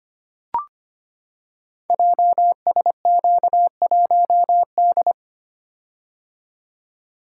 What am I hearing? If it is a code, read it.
Morse code, heard as a keyed tone, about 25 wpm, 700 Hz. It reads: JSQ1D
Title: Code Groups - 5 Letter:Numbers - 25wpm